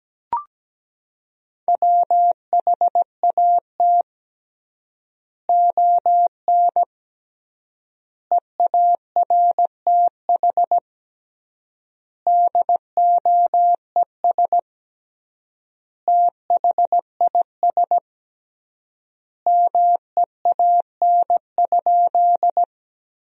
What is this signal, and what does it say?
Morse code, heard as a keyed tone, about 17 wpm, 700 Hz. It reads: WHAT ON EARTH DOES THIS MEAN?
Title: 3x Word-Spacing - Sentences from Top 200 Words - 17wpm